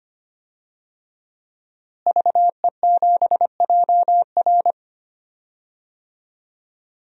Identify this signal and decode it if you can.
Morse code, heard as a keyed tone, about 25 wpm, 700 Hz. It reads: VE7JR
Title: DXCC-100 Call Signs - 5 Characters - Repeated 2x - 25wpm